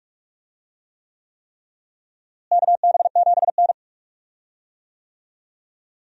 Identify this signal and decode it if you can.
Morse code, heard as a keyed tone, about 42 wpm, 700 Hz. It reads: KB7N
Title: USA Call Signs - 4 Characters - Repeated 3x - 45wpm